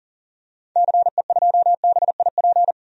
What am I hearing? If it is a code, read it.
Morse code, heard as a keyed tone, about 39 wpm, 700 Hz. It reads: CE2BIP
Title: DXCC-100 Call Signs - 6 Characters - 40wpm